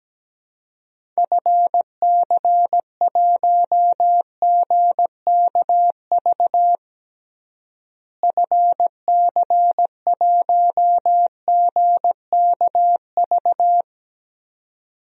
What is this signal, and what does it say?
Morse code, heard as a keyed tone, about 17 wpm, 700 Hz. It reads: FC1GKV FC1GKV
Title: DXCC-100 Call Signs - 6 Characters - Repeated 2x - 17wpm